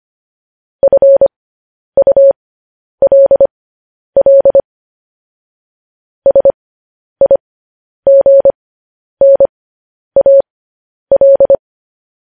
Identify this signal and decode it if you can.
Morse code, heard as a keyed tone, about 25 wpm, spaced as for 10 wpm, 550 Hz. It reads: FULL SIGNAL
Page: news-headlines-in-morse-code-at-10-wpm